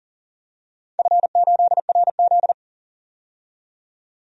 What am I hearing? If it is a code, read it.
Morse code, heard as a keyed tone, about 40 wpm, 700 Hz. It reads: F8RZ
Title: DXCC-100 Call Signs - 4 Characters - 40wpm